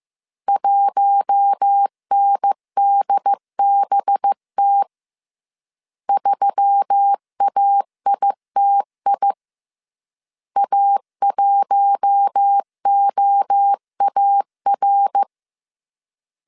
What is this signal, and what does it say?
Morse code, heard as a keyed tone, about 15 wpm, 800 Hz. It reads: 1NDBT 3AITI A1OAR